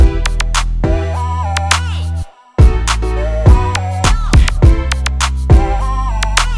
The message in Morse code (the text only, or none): none